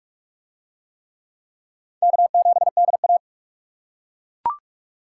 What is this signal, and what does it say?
Morse code, heard as a keyed tone, about 41 wpm, 700 Hz. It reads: K7DA